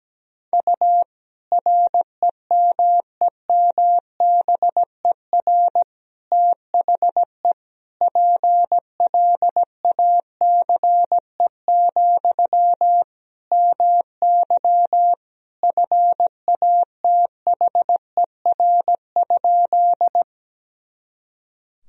Morse code, heard as a keyed tone, about 17 wpm, 700 Hz. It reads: U REMEMBER THE PLACE, MY FATHER?
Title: Sentences from Top 400 Words and Abbreviations 17wpm